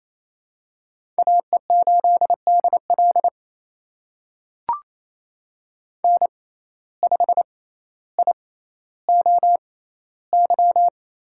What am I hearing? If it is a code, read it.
Morse code, heard as a keyed tone, about 28 wpm, spaced as for 14 wpm, 700 Hz. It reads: AE8DL N5IOY
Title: USA Call Signs - 5 Characters - 6x Character Spacing - 28wpm